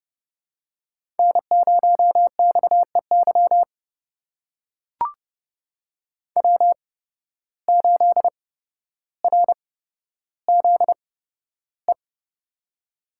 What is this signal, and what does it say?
Morse code, heard as a keyed tone, about 30 wpm, spaced as for 13 wpm, 700 Hz. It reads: N0XEY W8RZE
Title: USA Call Signs - 5 Characters - 8x Character Spacing - 30wpm